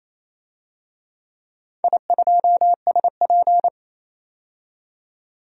Morse code, heard as a keyed tone, about 28 wpm, 700 Hz. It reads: I2SP
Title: DXCC-100 Call Signs - 4 Characters - 28wpm